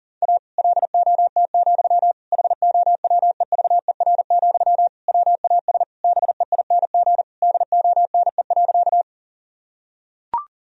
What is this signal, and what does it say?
Morse code, heard as a keyed tone, about 40 wpm, 700 Hz. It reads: A LOT, HOWEVER, WAS BEING DONE.
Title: Sentences from Top 1000 Words - Review Part 3 - 40wpm